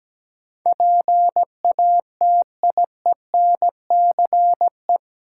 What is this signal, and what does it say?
Morse code, heard as a keyed tone, about 17 wpm, 700 Hz. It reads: PATIENCE
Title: Sentences from Top 1300 Words - New Word Drill - 17wpm